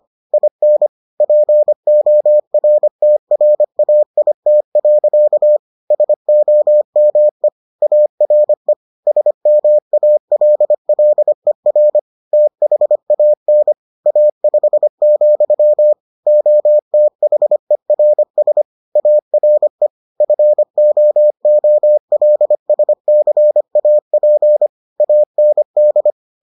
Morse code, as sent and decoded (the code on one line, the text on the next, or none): .. -. / .--. --- .-. - .-. .- .. - .-.-.- / ... --- -- . / .- .-. / ... -- .- .-.. .-.. .-. / - .... .- -. / .- ..... --..-- / --- - .... .-. ... / .- .-. / ..-. --- --- .-.. ... -.-. .- .--. / .- -. -..
IN PORTRAIT. SOME ARE SMALLER THAN A5, OTHERS ARE FOOLSCAP AND